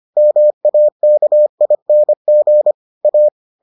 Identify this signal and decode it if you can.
Morse code, heard as a keyed tone, about 25 wpm, 600 Hz. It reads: MAKING A